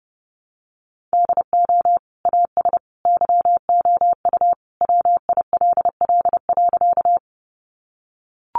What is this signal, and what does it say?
Morse code, heard as a keyed tone, about 30 wpm, 700 Hz. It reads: DO AS YOU WILL.
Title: Sentences from Top 100 Words 30wpm